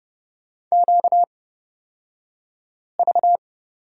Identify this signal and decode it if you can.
Morse code, heard as a keyed tone, about 31 wpm, spaced as for 5 wpm, 700 Hz. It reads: QV
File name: Lesson 38d - Letter q - Two Characters 30wpm